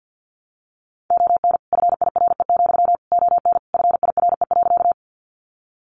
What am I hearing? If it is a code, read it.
Morse code, heard as a keyed tone, about 50 wpm, 700 Hz. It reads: ON FIRE, ON FIRE.
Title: Sentences from Top 400 Words 50wpm